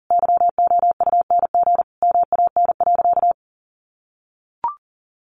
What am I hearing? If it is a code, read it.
Morse code, heard as a keyed tone, about 40 wpm, 700 Hz. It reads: YOUNG MAN.
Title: Sentences from Top 1000 Words 40wpm